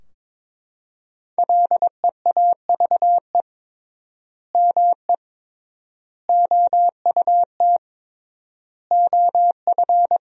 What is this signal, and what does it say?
Morse code, heard as a keyed tone, about 22 wpm, 700 Hz. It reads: LEAVE ME OUT OF
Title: 3x Word-Spacing - Sentences from Top 400 Words - 22wpm